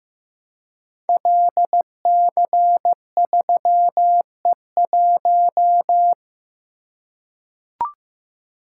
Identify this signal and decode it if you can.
Morse code, heard as a keyed tone, about 15 wpm, 700 Hz. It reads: LC3E1
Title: Code Groups - 5 Letter:Numbers - 15wpm